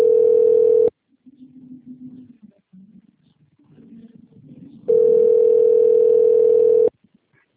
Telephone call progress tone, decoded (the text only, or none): ringback tone